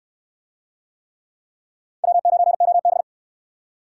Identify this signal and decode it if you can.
Morse code, heard as a keyed tone, about 65 wpm, 700 Hz. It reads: W9OB